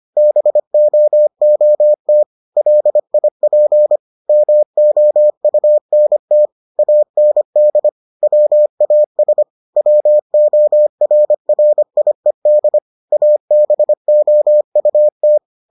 Morse code, heard as a keyed tone, about 25 wpm, 600 Hz. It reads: BOOT LIP MOUNT AND WAS WORRIED ABOUT